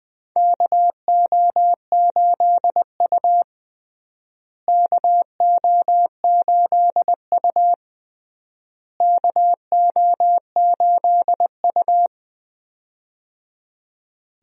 Morse code, heard as a keyed tone, about 20 wpm, 700 Hz. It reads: KO8U KO8U KO8U
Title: USA Call Signs - 4 Characters - Repeated 3x - 20wpm